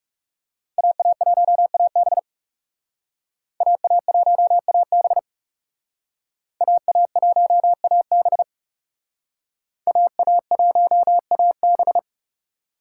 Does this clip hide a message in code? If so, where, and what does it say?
Yes: Morse code from 0.8 to 12.0 s, heard as a keyed tone, about 39 wpm, 700 Hz. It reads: AA1AB AA1AB AA1AB AA1AB